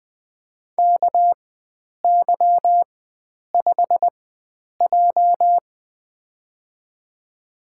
Morse code, heard as a keyed tone, about 20 wpm, spaced as for 9 wpm, 700 Hz. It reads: KY5J
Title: USA Call Signs - 4 Characters - 4x Character Spacing - 20wpm